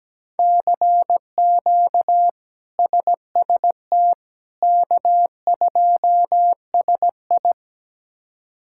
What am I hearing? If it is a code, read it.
Morse code, heard as a keyed tone, about 17 wpm, 700 Hz. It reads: CQ SST K2SI
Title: K1USN SST QSO Practice - 17wpm